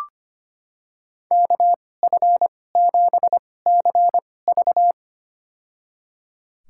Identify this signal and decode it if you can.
Morse code, heard as a keyed tone, about 25 wpm, spaced as for 18 wpm, 700 Hz. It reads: KF7CV
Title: USA Call Signs - 2x Character Spacing - 25wpm